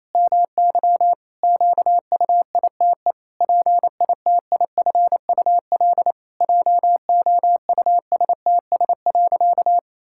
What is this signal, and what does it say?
Morse code, heard as a keyed tone, about 28 wpm, 700 Hz. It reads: MY QUITE PITIFUL JOUSTS.
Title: Pangram Challenge v2 - 28wpm